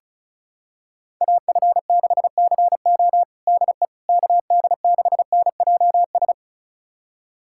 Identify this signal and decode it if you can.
Morse code, heard as a keyed tone, about 36 wpm, 700 Hz. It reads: AF6CO DE KD6NJS
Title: USA Callsign DE Callsign 35wpm